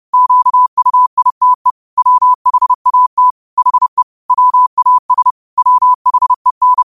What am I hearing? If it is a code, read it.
Morse code, heard as a keyed tone, about 30 wpm, 1000 Hz. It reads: QUITE WHAT HE WAS WHEN
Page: New quotes every day in morse code at 30 Words per minute.